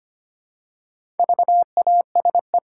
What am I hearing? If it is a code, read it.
Morse code, heard as a keyed tone, about 25 wpm, 700 Hz. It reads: VASE
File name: practice / 25wpm / Easy Words with Hints - 25wpm